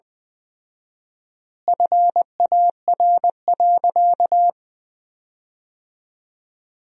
Morse code, heard as a keyed tone, about 20 wpm, 700 Hz. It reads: FAR.
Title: 4x Word-Spacing - Sentences from Top 400 Words - 20wpm